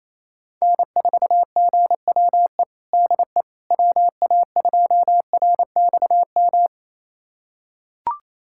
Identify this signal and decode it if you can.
Morse code, heard as a keyed tone, about 27 wpm, 700 Hz. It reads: N4GWE DE WA2RXM
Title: USA Callsign DE Callsign 28wpm